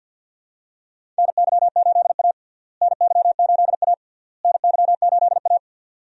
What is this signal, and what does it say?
Morse code, heard as a keyed tone, about 52 wpm, 700 Hz. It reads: NY8A NY8A NY8A
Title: USA Call Signs - 4 Characters - Repeated 3x - 50wpm